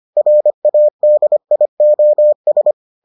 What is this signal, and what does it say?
Morse code, heard as a keyed tone, about 25 wpm, 600 Hz. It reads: RADIOS